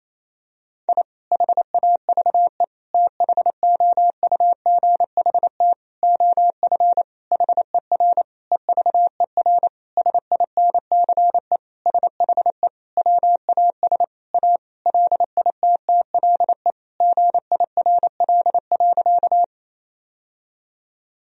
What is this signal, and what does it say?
Morse code, heard as a keyed tone, about 28 wpm, 700 Hz. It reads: I HAVE THOUGHT OF HER EVER SINCE SHE WAS A LITTLE GIRL.
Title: Sentences from Top 1000 Words - Review Part 1 - 28wpm